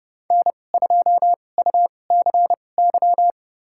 Morse code, heard as a keyed tone, about 30 wpm, spaced as for 22 wpm, 700 Hz. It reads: N2UCY
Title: USA Call Signs - 2x Character Spacing - 30wpm